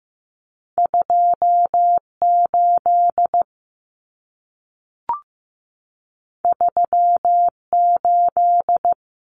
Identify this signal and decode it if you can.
Morse code, heard as a keyed tone, about 15 wpm, 700 Hz. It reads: 28 38
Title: Common Cut Numbers - 2-Digit 15wpm